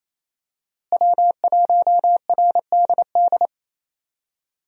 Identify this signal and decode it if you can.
Morse code, heard as a keyed tone, about 27 wpm, 700 Hz. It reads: W1RDD
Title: USA Call Signs - 5 Characters - Repeated 3x - 28wpm